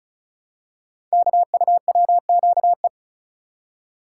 Morse code, heard as a keyed tone, about 36 wpm, 700 Hz. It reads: KUWQE
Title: Code Groups - 5 Letters - 35wpm